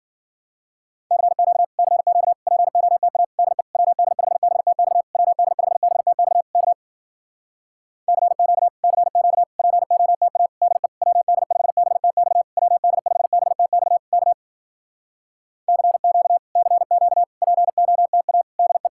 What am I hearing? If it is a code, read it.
Morse code, heard as a keyed tone, about 60 wpm, 700 Hz. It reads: CQ CQ POTA DE WD5BTX WD5BTX K CQ CQ POTA DE WD5BTX WD5BTX K CQ CQ POTA DE